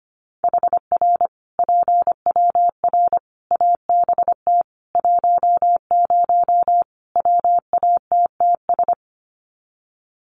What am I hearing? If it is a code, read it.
Morse code, heard as a keyed tone, about 25 wpm, 700 Hz. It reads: HR PWR ABT 10 WATTS
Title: Common QSO Phrases 25wpm